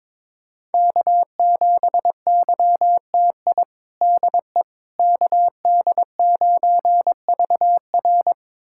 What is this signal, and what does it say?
Morse code, heard as a keyed tone, about 22 wpm, 700 Hz. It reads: K7YTI DE KD9VR
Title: USA Callsign DE Callsign - 5 Characters - 22wpm